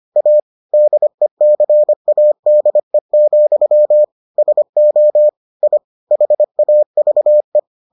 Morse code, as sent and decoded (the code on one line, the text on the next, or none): .- / -.. -.-. .- -.. --..-- / ... --- / .. / .... .- ...- .
A DECADE, SO I HAVE